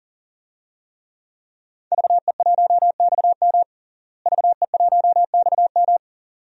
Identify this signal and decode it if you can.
Morse code, heard as a keyed tone, about 40 wpm, 700 Hz. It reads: VE1XM VE1XM
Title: DXCC-100 Call Signs - 5 Characters - Repeated 2x - 40wpm